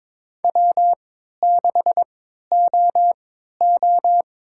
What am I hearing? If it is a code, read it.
Morse code, heard as a keyed tone, about 22 wpm, spaced as for 12 wpm, 700 Hz. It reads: W6OO
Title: USA Call Signs - 4 Characters - 3x Character Spacing - 22wpm